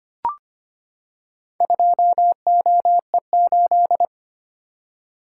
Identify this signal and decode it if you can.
Morse code, heard as a keyed tone, about 25 wpm, 690 Hz. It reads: 2OE8
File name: practice / 25wpm / Four Letter-Number Combinations - 25wpm